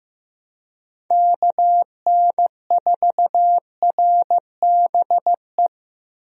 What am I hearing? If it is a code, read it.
Morse code, heard as a keyed tone, about 15 wpm, 700 Hz. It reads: KN4RBE
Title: USA Call Signs - 6 Characters - Repeated 3x - 15wpm